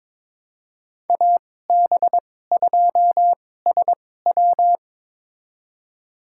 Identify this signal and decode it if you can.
Morse code, heard as a keyed tone, about 22 wpm, spaced as for 16 wpm, 700 Hz. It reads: AB2SW
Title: USA Call Signs - 5 Characters - 2x Character Spacing - 22wpm